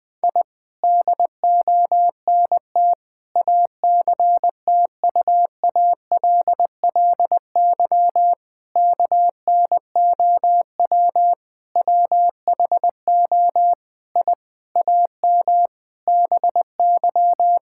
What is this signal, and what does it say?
Morse code, heard as a keyed tone, about 20 wpm, 700 Hz. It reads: I DONT ACTUALLY KNOW WHO I AM BY
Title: Sentences from Top 1000 Words - Review Part 5 - 20wpm